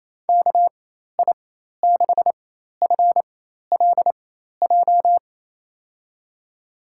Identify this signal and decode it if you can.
Morse code, heard as a keyed tone, about 27 wpm, spaced as for 13 wpm, 700 Hz. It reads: KI6FLJ
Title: USA Call Signs - 6 Characters - 4x Character Spacing - 28wpm